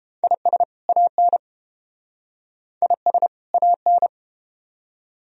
Morse code, heard as a keyed tone, about 30 wpm, 700 Hz. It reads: IS AN IS AN